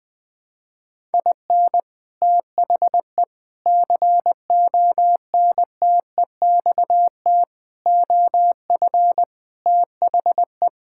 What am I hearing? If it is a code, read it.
Morse code, heard as a keyed tone, about 20 wpm, 700 Hz. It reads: IN THE CONTEXT OF THE
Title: Sentences with 5-Word Phrases - 20wpm